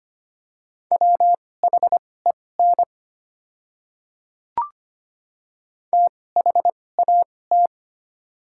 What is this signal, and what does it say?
Morse code, heard as a keyed tone, about 25 wpm, spaced as for 18 wpm, 700 Hz. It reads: WHEN THAT
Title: 2x Character Spacing - Top 300 Words - 25wpm